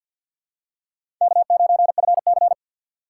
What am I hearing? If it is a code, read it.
Morse code, heard as a keyed tone, about 51 wpm, 700 Hz. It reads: K9FC
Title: USA Call Signs - 4 Characters - 3x Character Spacing - 50wpm